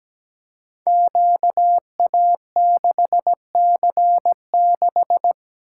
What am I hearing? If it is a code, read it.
Morse code, heard as a keyed tone, about 17 wpm, 700 Hz. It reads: QA6C6
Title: Code Groups - 5 Letter:Numbers - 17wpm